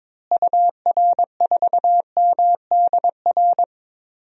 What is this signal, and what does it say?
Morse code, heard as a keyed tone, about 22 wpm, 700 Hz. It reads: UR4MDR
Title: DXCC-100 Call Signs - 6 Characters - Repeated 3x - 22wpm